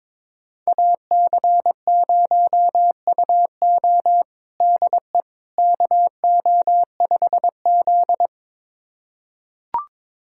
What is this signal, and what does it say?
Morse code, heard as a keyed tone, about 22 wpm, 700 Hz. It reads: AC0UO DE KO5Z
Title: USA Callsign DE Callsign 22wpm